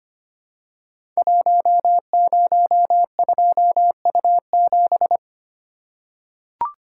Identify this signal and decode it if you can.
Morse code, heard as a keyed tone, about 25 wpm, 700 Hz. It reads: 102U7
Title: Code Groups - 5 Letter:Numbers - 25wpm